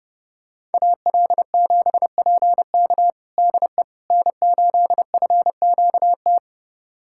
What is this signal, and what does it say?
Morse code, heard as a keyed tone, about 30 wpm, 700 Hz. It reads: AL7PK DE N8FQT
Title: USA Callsign DE Callsign - 5 Characters - 30wpm